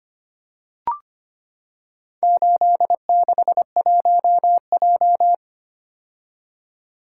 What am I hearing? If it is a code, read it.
Morse code, heard as a keyed tone, about 25 wpm, 700 Hz. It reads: 861J
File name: practice / 25wpm / Four Letter-Number Combinations - 25wpm